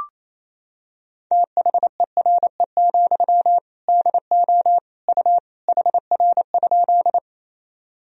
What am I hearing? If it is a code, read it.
Morse code, heard as a keyed tone, about 28 wpm, 700 Hz. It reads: THERE, DO U HR?
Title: Sentences from Top 300 Words and Abbreviations 28wpm